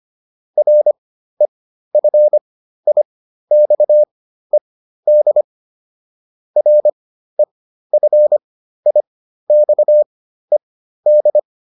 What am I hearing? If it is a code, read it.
Morse code, heard as a keyed tone, about 25 wpm, spaced as for 13 wpm, 600 Hz. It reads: REFIXED REFIXED